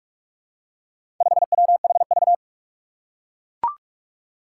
Morse code, heard as a keyed tone, about 44 wpm, 700 Hz. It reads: 5WHV
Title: Four Challenging Letter-Number Combinations 45wpm